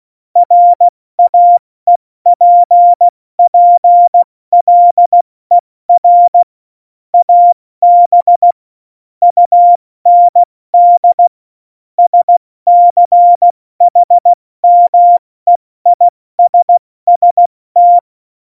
Morse code, heard as a keyed tone, about 16 wpm, 700 Hz. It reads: RAEPPLER AB UND SCHMEISST
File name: Morsekurs / Witze / Witz 15. Oktober 2019, 70 F80